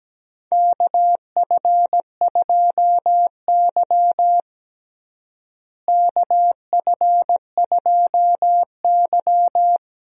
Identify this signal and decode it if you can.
Morse code, heard as a keyed tone, about 17 wpm, 700 Hz. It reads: KF2Y KF2Y
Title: USA Call Signs - 4 Characters - Repeated 2x - 17wpm